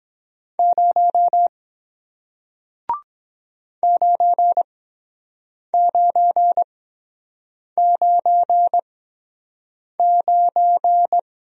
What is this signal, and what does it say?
Morse code, heard as a keyed tone, about 22 wpm, spaced as for 7 wpm, 700 Hz. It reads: 0 9999